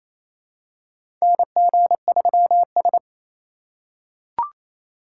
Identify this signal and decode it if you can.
Morse code, heard as a keyed tone, about 28 wpm, 700 Hz. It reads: NG3S
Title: USA Call Signs - 4 Characters - 2x Character Spacing - 28wpm